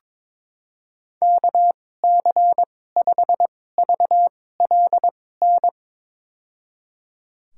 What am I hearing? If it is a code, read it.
Morse code, heard as a keyed tone, about 22 wpm, spaced as for 16 wpm, 700 Hz. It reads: KC5VLN
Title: USA Call Signs - 6 Characters - 2x Character Spacing - 22wpm